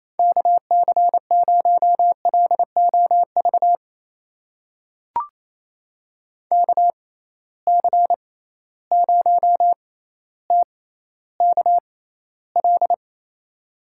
Morse code, heard as a keyed tone, about 28 wpm, spaced as for 14 wpm, 700 Hz. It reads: KC0LOV KC0TKL
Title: USA Call Signs - 6 Characters - 6x Character Spacing - 28wpm